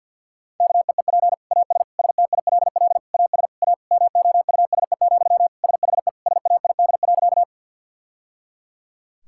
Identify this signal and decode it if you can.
Morse code, heard as a keyed tone, about 50 wpm, 700 Hz. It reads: KEEP AS STILL AS A MOUSE, SHE SAID.
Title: Sentences from Top 1000 Words - Review Part 5 - 50wpm